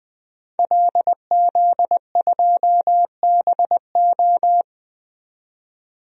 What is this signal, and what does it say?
Morse code, heard as a keyed tone, about 20 wpm, 700 Hz. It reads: LZ2BO
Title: DXCC-100 Call Signs - 5 Characters - 20wpm